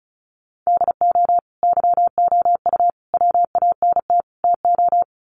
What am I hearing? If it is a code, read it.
Morse code, heard as a keyed tone, about 34 wpm, 700 Hz. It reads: DO YOU WANT TO
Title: Sets of 4 Words 35wpm